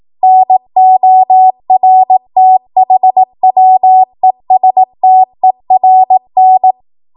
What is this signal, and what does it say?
Morse code, heard as a keyed tone, about 18 wpm, 750 Hz. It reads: NORTHWESTERN